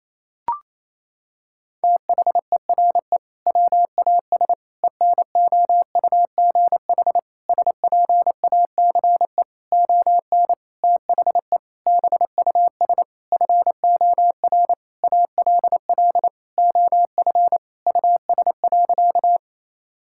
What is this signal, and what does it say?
Morse code, heard as a keyed tone, about 27 wpm, 700 Hz. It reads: THERE WAS ENOUGH SPACE ON THE BUS FOR ALL OF US.
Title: Sentences from Top 1000 Words - Review Part 6 - 28wpm